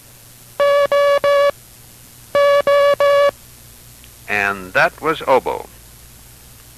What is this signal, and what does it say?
Morse code, heard as a keyed tone, about 18 wpm, spaced as for 8 wpm, 570 Hz, with speech. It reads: OOEE